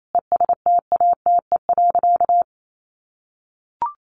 Morse code, heard as a keyed tone, about 28 wpm, 700 Hz. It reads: ESTATE.
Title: Sentences from Top 1200 Words 28wpm